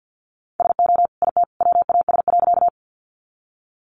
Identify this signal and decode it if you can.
Morse code, heard as a keyed tone, about 50 wpm, 700 Hz. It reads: SO IT WAS.
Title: Sentences from Top 100 Words 50wpm